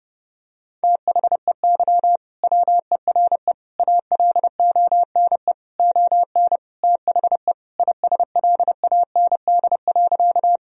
Morse code, heard as a keyed tone, about 32 wpm, 700 Hz. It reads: THEY WERE ALONE ON THE ISLAND.
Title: Sentences from Top 1000 Words - Review Part 3 - 30wpm